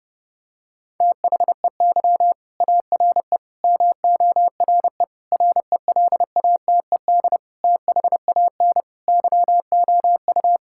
Morse code, heard as a keyed tone, about 32 wpm, 700 Hz. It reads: THEY ARE MORE RELATED THAN YOU
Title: Sentences from Top 1000 Words - Review Part 5 - 30wpm